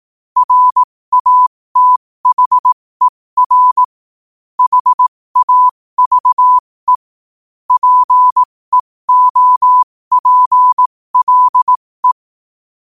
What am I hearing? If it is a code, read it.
Morse code, heard as a keyed tone, about 18 wpm, spaced as for 15 wpm, 1000 Hz. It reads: RATHER HAVE PEOPLE
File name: qotdfiles / 15wpm